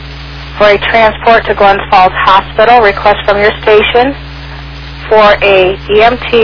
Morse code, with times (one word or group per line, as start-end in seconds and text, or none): none